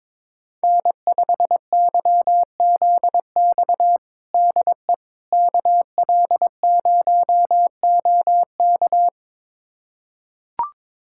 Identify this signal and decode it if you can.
Morse code, heard as a keyed tone, about 22 wpm, 700 Hz. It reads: N5YZX DE KL0OK
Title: USA Callsign DE Callsign - 5 Characters - 22wpm